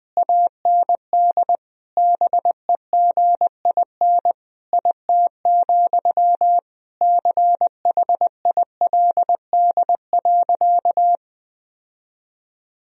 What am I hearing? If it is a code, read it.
Morse code, heard as a keyed tone, about 20 wpm, 700 Hz. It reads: AND BEGIN IT, CHILD.
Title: Sentences from Top 1000 Words - Review Part 1 - 20wpm